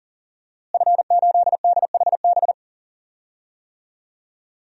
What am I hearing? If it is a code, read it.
Morse code, heard as a keyed tone, about 40 wpm, 700 Hz. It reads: F8DHB
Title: DXCC-100 Call Signs - 5 Characters - Repeated 3x - 40wpm